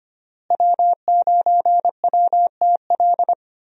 Morse code, heard as a keyed tone, about 25 wpm, 700 Hz. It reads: W9WTL
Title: USA Call Signs - 5 Characters - Repeated 2x - 25wpm